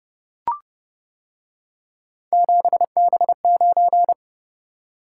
Morse code, heard as a keyed tone, about 30 wpm, 700 Hz. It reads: 7B9